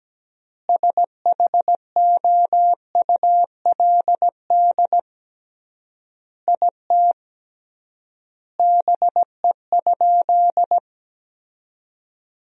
Morse code, heard as a keyed tone, about 17 wpm, 700 Hz. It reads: SHOULD IT BE?
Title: 3x Word-Spacing - Sentences from Top 300 Words - 17wpm